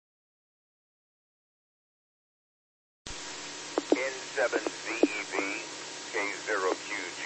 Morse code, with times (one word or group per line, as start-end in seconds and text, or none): none